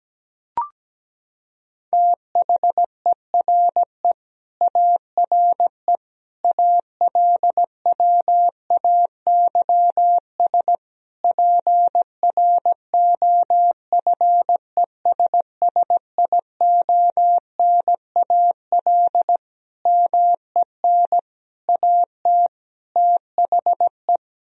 Morse code, heard as a keyed tone, about 17 wpm, 700 Hz. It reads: THERE ARE ALWAYS PROFESSIONAL MEN AT THE
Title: Sentences from Top 1000 Words - Review Part 4 - 17wpm